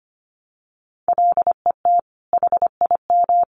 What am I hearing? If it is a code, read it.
Morse code, heard as a keyed tone, about 25 wpm, 700 Hz. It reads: LET HIM
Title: Sentences from Top 300 Words 25wpm